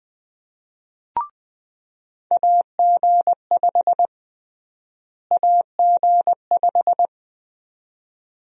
Morse code, heard as a keyed tone, about 20 wpm, 700 Hz. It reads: AG5 AG5
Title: USA Call Sign Prefixes - Repeated 2x - 20wpm